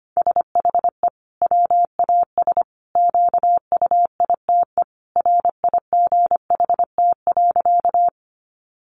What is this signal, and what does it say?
Morse code, heard as a keyed tone, about 25 wpm, 700 Hz. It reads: SHE WAS QUITE RIGHT.
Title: Sentences from Top 700 Words 25wpm